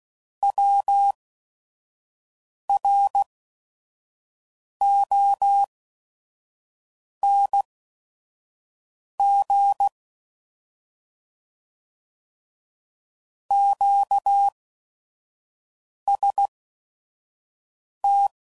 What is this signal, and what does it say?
Morse code, heard as a keyed tone, about 16 wpm, spaced as for 5 wpm, 800 Hz. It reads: WRONG QST
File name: Mixed / Misc / Words